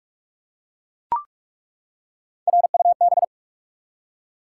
Morse code, heard as a keyed tone, about 43 wpm, 700 Hz. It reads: RUB
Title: Three Letter Words - 45wpm